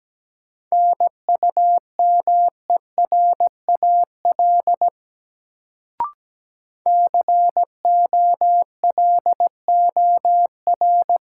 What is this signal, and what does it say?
Morse code, heard as a keyed tone, about 17 wpm, 700 Hz. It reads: NUMERAL COLOR
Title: Sentences from Top 1000 Words - Review Part 2 - New Word Drill - 17wpm